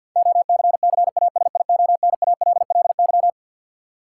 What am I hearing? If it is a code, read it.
Morse code, heard as a keyed tone, about 52 wpm, 700 Hz. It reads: OCCASIONALLY
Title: Sentences from Top 1300 Words - New Word Drill - 50wpm